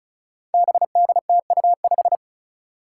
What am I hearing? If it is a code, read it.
Morse code, heard as a keyed tone, about 36 wpm, 700 Hz. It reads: BDTU5
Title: Five Challenging Letter-Number Combinations 35wpm